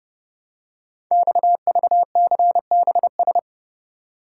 Morse code, heard as a keyed tone, about 30 wpm, 700 Hz. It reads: XVCBS
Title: Code Groups - 5 Letters - 30wpm